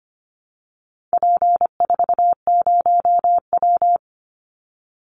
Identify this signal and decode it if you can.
Morse code, heard as a keyed tone, about 25 wpm, 700 Hz. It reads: P40W
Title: Call Signs Repeated 2x - 25wpm